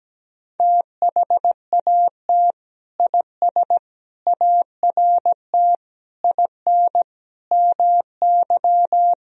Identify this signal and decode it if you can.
Morse code, heard as a keyed tone, about 17 wpm, 700 Hz. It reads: THAT IS ART IN MY
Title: Sentences from Top 1000 Words - Review Part 3 - 17wpm